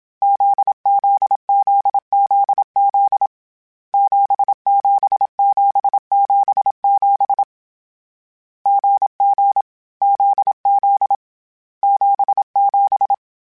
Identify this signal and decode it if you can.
Morse code, heard as a keyed tone, about 27 wpm, 800 Hz. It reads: ZZZZZ 77777 GG ZZ 77
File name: Difficult Combos / Difficult Combos-mp3s / GZ GZ7 @25